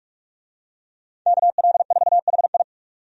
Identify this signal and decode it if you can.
Morse code, heard as a keyed tone, about 42 wpm, 700 Hz. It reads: KL4HI
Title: USA Call Signs - 5 Characters - 45wpm